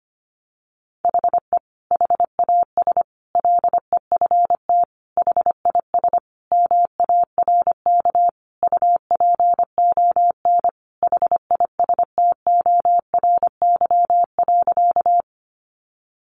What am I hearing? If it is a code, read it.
Morse code, heard as a keyed tone, about 25 wpm, 700 Hz. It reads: HE HAS LEFT HIS MARK UPON HISTORY.
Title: Sentences from Top 1100 Words 25wpm